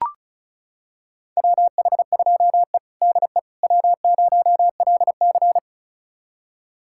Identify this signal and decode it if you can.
Morse code, heard as a keyed tone, about 34 wpm, 700 Hz. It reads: WH2E DE W0LC
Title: USA Callsign DE Callsign - 4 Characters - 35wpm